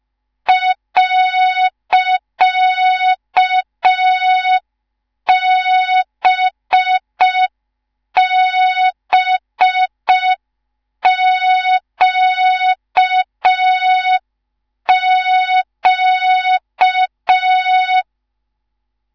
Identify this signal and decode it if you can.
Morse code, heard as a keyed tone, about 5 wpm, 3700 Hz. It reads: .BBQQ
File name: Morze-HA2MN / hang / 19 GYAKORLAT